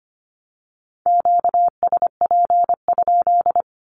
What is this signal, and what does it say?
Morse code, heard as a keyed tone, about 25 wpm, 700 Hz. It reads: QSP?
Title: Q-Codes 25wpm